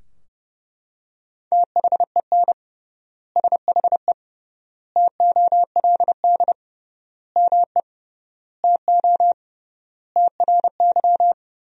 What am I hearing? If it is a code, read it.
Morse code, heard as a keyed tone, about 30 wpm, 700 Hz. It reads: THEN SHE TOLD ME TO TRY
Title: 3x Word-Spacing - Sentences from Top 400 Words - 30wpm